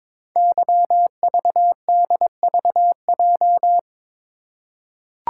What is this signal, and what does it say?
Morse code, heard as a keyed tone, about 23 wpm, 700 Hz. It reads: YVDVJ
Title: Code Groups - 5 Letters - 22wpm